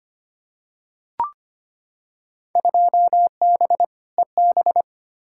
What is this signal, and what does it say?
Morse code, heard as a keyed tone, about 25 wpm, 700 Hz. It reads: 2B EB
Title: Field Day Exchanges - 25wpm